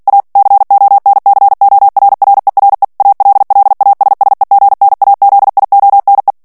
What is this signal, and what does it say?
Morse code, heard as a keyed tone, about 48 wpm, 800 Hz. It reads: A CONCORRERE ALLASSEGNAZIONE
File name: ITALIEN texte clair 2 vitesse 45 mots minute